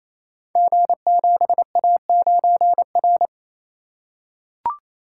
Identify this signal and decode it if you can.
Morse code, heard as a keyed tone, about 27 wpm, 700 Hz. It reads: G7A9R
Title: Code Groups - 5 Letter:Numbers - 28wpm